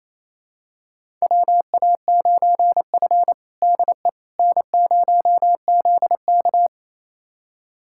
Morse code, heard as a keyed tone, about 28 wpm, 700 Hz. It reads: WA9F DE N0ZK
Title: USA Callsign DE Callsign - 4 Characters - 28wpm